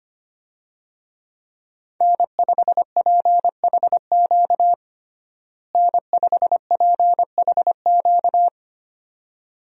Morse code, heard as a keyed tone, about 25 wpm, 700 Hz. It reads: N5PHQ N5PHQ
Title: USA Call Signs - 5 Characters - Repeated 2x - 25wpm